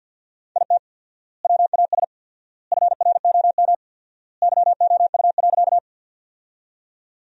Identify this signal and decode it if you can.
Morse code, heard as a keyed tone, about 50 wpm, 700 Hz. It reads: IT WAS FROM YOU.
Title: 4x Word-Spacing - Sentences from Top 100 Words - 50wpm